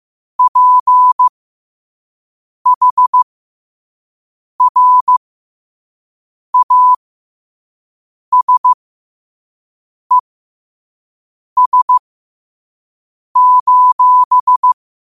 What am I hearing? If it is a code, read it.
Morse code, heard as a keyed tone, about 15 wpm, spaced as for 5 wpm, 1000 Hz. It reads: PHRASES:
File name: qotdfiles / 5wpm